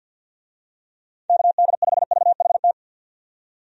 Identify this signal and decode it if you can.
Morse code, heard as a keyed tone, about 51 wpm, 700 Hz. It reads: KD5VHT
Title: USA Call Signs - 6 Characters - 50wpm